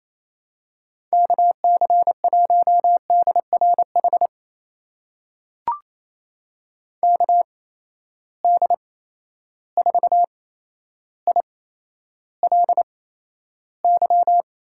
Morse code, heard as a keyed tone, about 28 wpm, spaced as for 12 wpm, 700 Hz. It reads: KC1DRH KD4ILY
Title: USA Call Signs - 8x Character Spacing - 28wpm